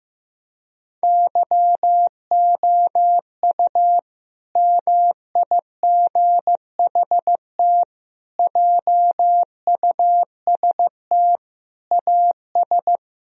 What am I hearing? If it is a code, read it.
Morse code, heard as a keyed tone, about 15 wpm, 700 Hz. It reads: YOU MIGHT JUST AS
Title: Sentences from Top 1000 Words - Review Part 2 - 15wpm